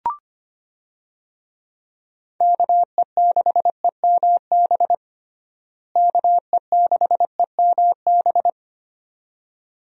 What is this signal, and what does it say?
Morse code, heard as a keyed tone, about 25 wpm, 700 Hz. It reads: KE6EMB KE6EMB
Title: USA Call Signs - 6 Characters - Repeated 2x - 25wpm